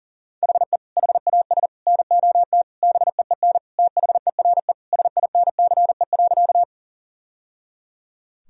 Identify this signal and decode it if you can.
Morse code, heard as a keyed tone, about 37 wpm, 700 Hz. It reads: HE HAS NOT BEEN THERE SINCE.
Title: Sentences from Top 1000 Words - Review Part 1 - 40wpm